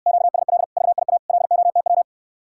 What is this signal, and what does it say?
Morse code, heard as a keyed tone, about 69 wpm, 700 Hz. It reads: QSL VIA LOTW